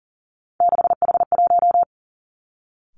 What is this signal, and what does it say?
Morse code, heard as a keyed tone, about 42 wpm, 690 Hz. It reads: 6H1
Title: Three Challenging Letter-Number Combinations 40wpm